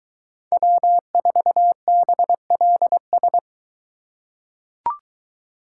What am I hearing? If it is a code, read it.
Morse code, heard as a keyed tone, about 22 wpm, 700 Hz. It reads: W4BLS